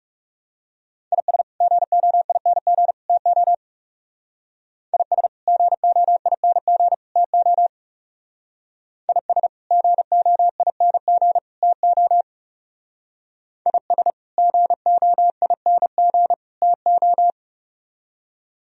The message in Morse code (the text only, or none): IS GOING TO IS GOING TO IS GOING TO IS GOING TO